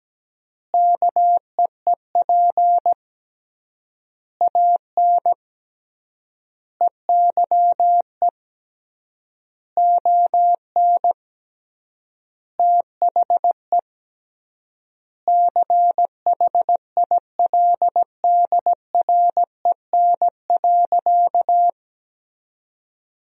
Morse code, heard as a keyed tone, about 17 wpm, 700 Hz. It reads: KEEP AN EYE ON THE CHILDREN.
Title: 3x Word-Spacing - Sentences from Top 300 Words - 17wpm